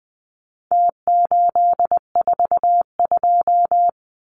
Morse code, heard as a keyed tone, about 20 wpm, 700 Hz. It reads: T842
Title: Numbers - Contest Style - 4-Digit 20wpm